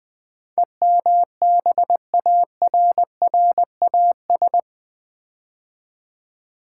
Morse code, heard as a keyed tone, about 20 wpm, 700 Hz. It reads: EMBARRAS
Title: Build A Word - 4 Letters At A Time - 12-Letter Words - 20wpm